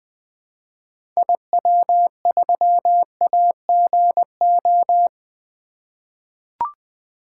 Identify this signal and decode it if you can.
Morse code, heard as a keyed tone, about 20 wpm, 700 Hz. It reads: IW3AGO
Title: 2023 SOTA Activator Call Signs - 20wpm